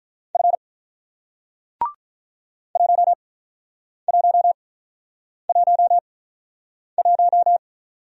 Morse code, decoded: F 1111

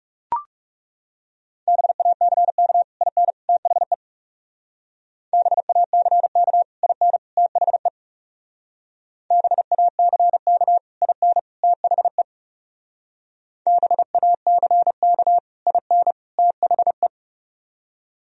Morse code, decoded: BACK IN THE BACK IN THE BACK IN THE BACK IN THE

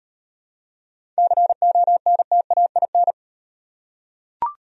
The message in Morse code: CONTAIN

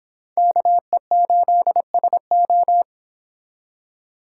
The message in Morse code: KE8SO

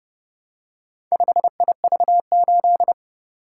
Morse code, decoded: 5IV8

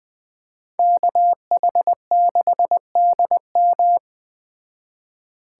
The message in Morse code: KH6DM

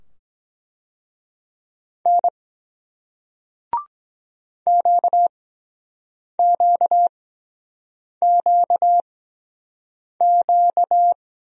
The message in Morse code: N QQQQ

